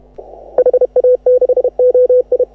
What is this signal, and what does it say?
Morse code, heard as a keyed tone, about 32 wpm, 500 Hz. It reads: HA6OI
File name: AUDIO2018 / HA6OI CW